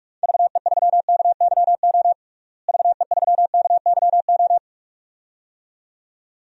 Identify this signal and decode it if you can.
Morse code, heard as a keyed tone, about 46 wpm, 700 Hz. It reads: VE3KYO VE3KYO
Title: DXCC-100 Call Signs - 6 Characters - Repeated 2x - 45wpm